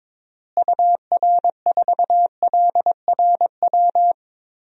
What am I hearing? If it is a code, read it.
Morse code, heard as a keyed tone, about 21 wpm, 700 Hz. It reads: UR4LRW
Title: DXCC-100 Call Signs - 6 Characters - Repeated 3x - 22wpm